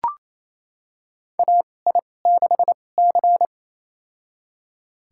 Morse code, heard as a keyed tone, about 28 wpm, spaced as for 20 wpm, 700 Hz. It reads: AI6C
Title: USA Call Signs - 4 Characters - 2x Character Spacing - 28wpm